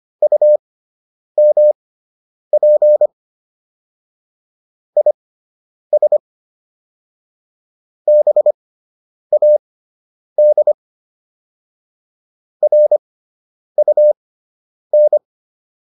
Morse code, heard as a keyed tone, about 25 wpm, spaced as for 9 wpm, 600 Hz. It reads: UMP IS BAD RUN